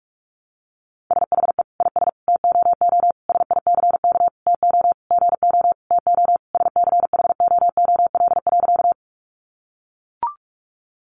Morse code, heard as a keyed tone, about 44 wpm, 700 Hz. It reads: SHE IS TOO SICK TO GO TO SCHOOL.
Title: Easy Phrases 45wpm